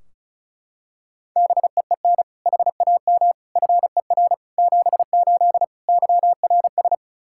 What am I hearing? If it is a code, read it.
Morse code, heard as a keyed tone, about 36 wpm, 700 Hz. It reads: BEEN HAM FER 78 YRS
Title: Practice QSO - Exchange 3 - More Info - 35wpm